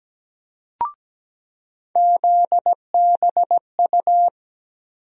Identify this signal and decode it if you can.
Morse code, heard as a keyed tone, about 17 wpm, 700 Hz. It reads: ZBU
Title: USA Call Sign Postfixes - 3 Letters 17wpm